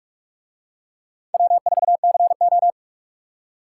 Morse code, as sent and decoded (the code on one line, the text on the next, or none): .-- ....- -.-. ---
W4CO